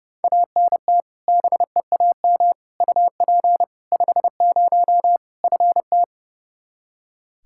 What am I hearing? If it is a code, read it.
Morse code, heard as a keyed tone, about 32 wpm, 700 Hz. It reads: ANT BEAM UP 50 FT
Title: Practice QSO - Exchange 2 - Info - 30wpm